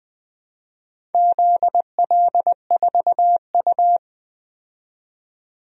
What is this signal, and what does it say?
Morse code, heard as a keyed tone, about 20 wpm, 700 Hz. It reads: ZL4U
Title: DXCC-100 Call Signs - 4 Characters - 20wpm